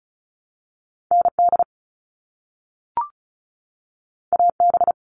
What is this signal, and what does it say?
Morse code, heard as a keyed tone, about 36 wpm, 700 Hz. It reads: ND AB
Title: All Cut Numbers - 2-Digit 35wpm